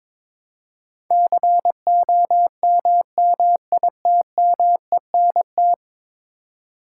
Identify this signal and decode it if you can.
Morse code, heard as a keyed tone, about 22 wpm, 700 Hz. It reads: COMMITMENT
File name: practice / 22wpm / 10 Letter Words - First 3 Letters Given - 22wpm